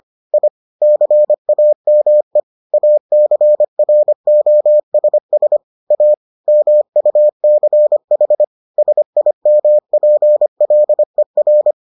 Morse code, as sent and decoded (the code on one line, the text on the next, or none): .. / -.-. .- -- . / .- -.-. .-. --- ... ... / .- / -- ..- -.-. .... / ... .. -- .--. .-.. .-.
I CAME ACROSS A MUCH SIMPLER